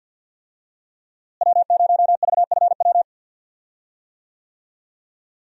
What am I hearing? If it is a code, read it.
Morse code, heard as a keyed tone, about 46 wpm, 700 Hz. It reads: W0VFW
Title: USA Call Signs - 5 Characters - Repeated 2x - 50wpm